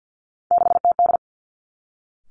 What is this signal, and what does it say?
Morse code, heard as a keyed tone, about 50 wpm, 700 Hz. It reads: 6TD
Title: Three Challenging Letter-Number Combinations 50wpm